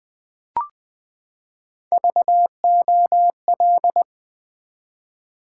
Morse code, heard as a keyed tone, about 20 wpm, 700 Hz. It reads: VOL
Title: Build A Word - 3 Letters At A Time - 9-Letter Words - 20wpm